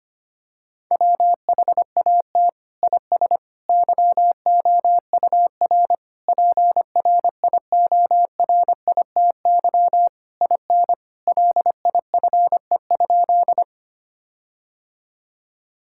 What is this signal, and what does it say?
Morse code, heard as a keyed tone, about 25 wpm, 700 Hz. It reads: WHAT IS YOUR PRIORITY IN LIFE?
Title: Sentences from Top 1600 Words 25wpm